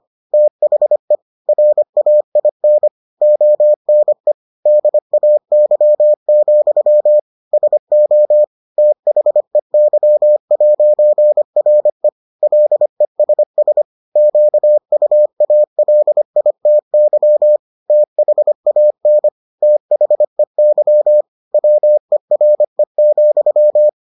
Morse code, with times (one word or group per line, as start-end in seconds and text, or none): 0.3-1.2 THE
1.5-2.9 RAIN
3.2-4.3 ONE
4.7-7.2 DAY,
7.5-8.4 SO
8.8-12.1 THEY'RE
12.4-13.8 LESS
14.2-17.6 QUALITY
17.9-19.3 THAN
19.6-21.2 THEY
21.5-23.9 WERE,